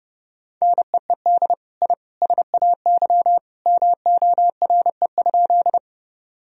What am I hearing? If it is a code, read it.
Morse code, heard as a keyed tone, about 30 wpm, 700 Hz. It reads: NEED I SAY MORE?
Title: Sentences from Top 1000 Words - Review Part 1 - 30wpm